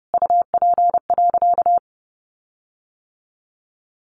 Morse code, heard as a keyed tone, about 28 wpm, 700 Hz. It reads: UP.
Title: Sentences from Top 1300 Words 30wpm